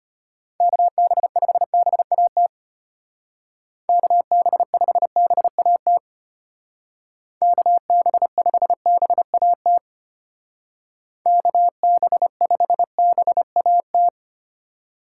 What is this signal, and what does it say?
Morse code, heard as a keyed tone, about 33 wpm, 700 Hz. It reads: KB5BAT KB5BAT KB5BAT KB5BAT